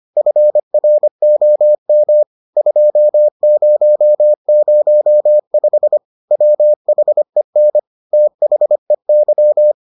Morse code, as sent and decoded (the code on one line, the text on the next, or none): ..-. .-. --- -- / ..--- ----- ----- ..... / .-- .... -. / - .... -.--
FROM 2005 WHEN THEY